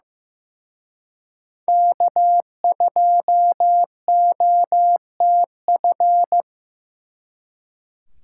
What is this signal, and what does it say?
Morse code, heard as a keyed tone, about 18 wpm, 700 Hz. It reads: K2OTF
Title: Speed Racing - USA 1x3 Call Signs 15wpm